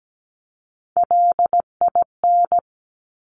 Morse code, heard as a keyed tone, about 17 wpm, 700 Hz. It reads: LIN
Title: Top Three Letter Combinations in Words 17wpm